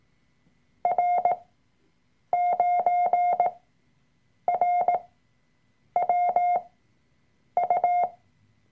Morse code, heard as a keyed tone, about 18 wpm, spaced as for 7 wpm, 710 Hz. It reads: R9RWU
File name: Workshop Dateien / CW-Online-Kurs / Lektion 003